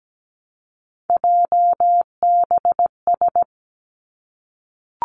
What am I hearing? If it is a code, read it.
Morse code, heard as a keyed tone, about 17 wpm, 700 Hz. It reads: JBS
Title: Three Challenging Letter-Number Combinations 17wpm